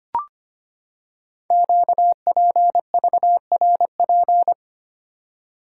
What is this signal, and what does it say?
Morse code, heard as a keyed tone, about 25 wpm, 700 Hz. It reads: QPVRP
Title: Code Groups - 5 Letters - 25wpm